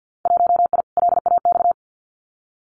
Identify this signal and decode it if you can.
Morse code, heard as a keyed tone, about 50 wpm, 700 Hz. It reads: 1I LAX
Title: Winter Field Day Exchanges Encore 50wpm